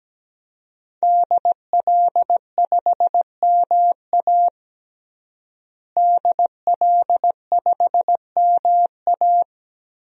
Morse code, heard as a keyed tone, about 17 wpm, 700 Hz. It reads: DL5MA DL5MA
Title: DXCC-100 Call Signs - 5 Characters - Repeated 3x - 17wpm